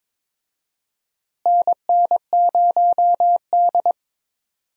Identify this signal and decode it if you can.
Morse code, heard as a keyed tone, about 22 wpm, 700 Hz. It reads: NN0D
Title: USA Call Signs - 4 Characters - Repeated 2x - 22wpm